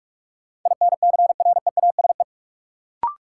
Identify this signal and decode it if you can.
Morse code, heard as a keyed tone, about 43 wpm, 700 Hz. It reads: INCREASE
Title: Sets of 5 Words - New Word Drill - 45wpm